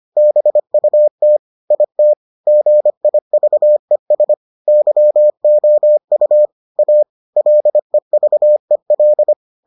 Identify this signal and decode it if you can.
Morse code, heard as a keyed tone, about 25 wpm, 600 Hz. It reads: BUT IT GIVES YOU A LEVEL